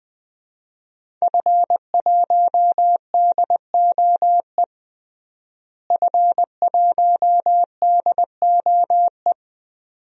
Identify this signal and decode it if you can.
Morse code, heard as a keyed tone, about 21 wpm, 700 Hz. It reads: F1DOE F1DOE
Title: DXCC-100 Call Signs - 5 Characters - Repeated 2x - 20wpm